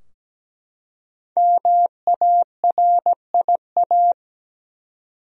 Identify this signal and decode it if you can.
Morse code, heard as a keyed tone, about 17 wpm, 700 Hz. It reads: MARIA
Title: USA Common Names - 17wpm